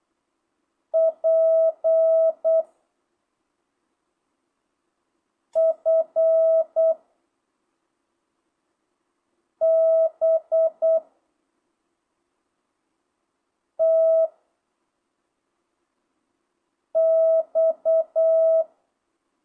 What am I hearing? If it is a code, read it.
Morse code, heard as a keyed tone, about 8 wpm, 660 Hz. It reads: P F B T X
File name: jh8cbh / 2025cw